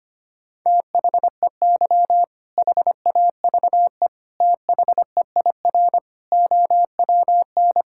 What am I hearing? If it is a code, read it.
Morse code, heard as a keyed tone, about 25 wpm, 700 Hz. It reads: THEY HAVE THEIR OWN
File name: practice / 25wpm / All-Age Appropriate Jokes - 25wpm